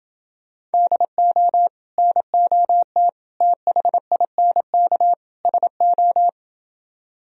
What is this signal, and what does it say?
Morse code, heard as a keyed tone, about 29 wpm, 700 Hz. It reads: DO NOT THINK SO